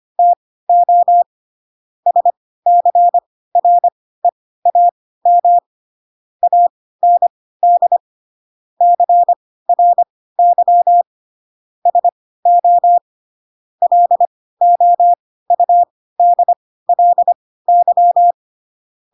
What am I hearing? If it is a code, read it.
Morse code, heard as a keyed tone, about 25 wpm, spaced as for 16 wpm, 700 Hz. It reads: TO SCREAM AND CRY SO LOUDLY
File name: Pinocchio / Pinocchio ch7 10 25 ik1hge